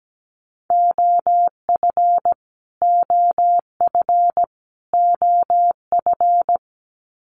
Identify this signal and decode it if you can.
Morse code, heard as a keyed tone, about 17 wpm, 700 Hz. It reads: OF OF OF
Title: Top 200 Words Repeated 17wpm